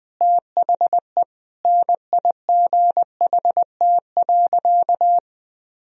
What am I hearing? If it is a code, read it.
Morse code, heard as a keyed tone, about 20 wpm, 700 Hz. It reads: THE NIGHT.
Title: Sentences from Top 1000 Words - Review Part 4 - 20wpm